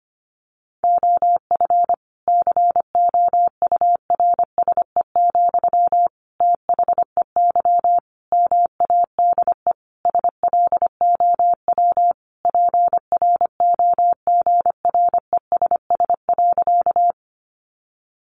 Morse code, heard as a keyed tone, about 25 wpm, 700 Hz. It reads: OF COURSE, THEY MADE SLOW PROGRESS.
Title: Sentences from Top 1400 Words 25wpm